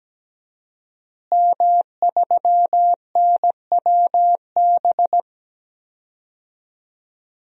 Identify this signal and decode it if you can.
Morse code, heard as a keyed tone, about 17 wpm, 700 Hz. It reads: M3NWB
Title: DXCC-100 Call Signs - 5 Characters - Repeated 3x - 17wpm